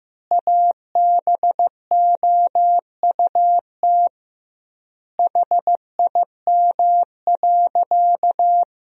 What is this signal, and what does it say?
Morse code, heard as a keyed tone, about 16 wpm, 700 Hz. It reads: ABOUT HIM.
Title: 2x Word-Spacing - Sentences from Top 200 Words - 15wpm